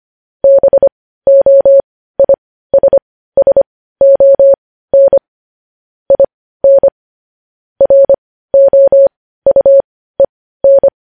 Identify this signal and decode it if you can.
Morse code, heard as a keyed tone, about 25 wpm, spaced as for 15 wpm, 550 Hz. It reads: BOISSON IN ROUEN
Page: news headlines in morse code at 15 wpm